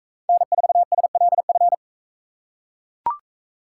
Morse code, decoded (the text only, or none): N4SLF